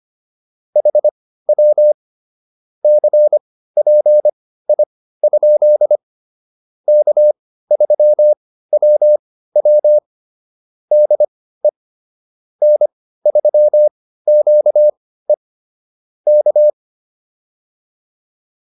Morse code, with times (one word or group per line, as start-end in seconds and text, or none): 0.8-1.9 HW
2.8-6.0 CPI?
6.9-10.0 K3WW
10.9-11.7 DE
12.6-15.3 N3QE
16.3-16.7 K